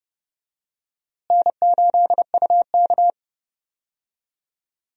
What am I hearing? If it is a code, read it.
Morse code, heard as a keyed tone, about 31 wpm, 700 Hz. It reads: N8UK
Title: USA Call Signs - 4 Characters - Repeated 3x - 30wpm